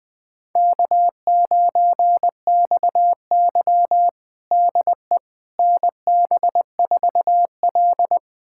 Morse code, heard as a keyed tone, about 20 wpm, 700 Hz. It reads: K9XY DE NB4L
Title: USA Callsign DE Callsign - 4 Characters - 20wpm